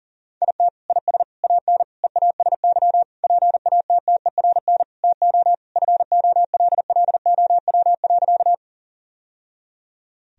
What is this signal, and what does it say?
Morse code, heard as a keyed tone, about 40 wpm, 700 Hz. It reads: IT IS AN EASY PATTERN TO FOLLOW.
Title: Sentences from Top 1000 Words - Review Part 2 - 40wpm